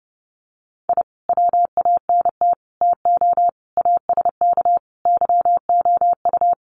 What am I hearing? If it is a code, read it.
Morse code, heard as a keyed tone, about 30 wpm, 700 Hz. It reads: I WANT TO ASK YOU
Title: Sets of 5 Words 30wpm